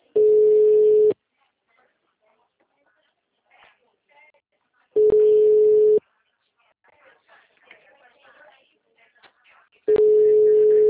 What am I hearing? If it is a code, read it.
Ringback tone (Europe, 425 Hz).